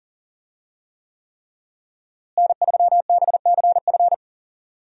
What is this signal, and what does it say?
Morse code, heard as a keyed tone, about 40 wpm, 700 Hz. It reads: N3BCF
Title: USA Call Signs - 5 Characters - Repeated 3x - 40wpm